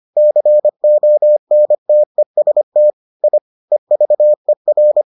Morse code, heard as a keyed tone, about 25 wpm, 600 Hz. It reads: CONTEST I EVER